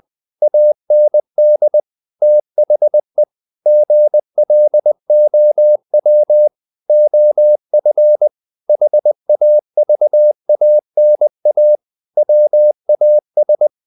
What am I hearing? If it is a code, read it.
Morse code, heard as a keyed tone, about 20 wpm, 600 Hz. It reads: AND THE GLOW OF HAVANA WAS
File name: the old man and the sea / the old man and the sea part 1 20wpm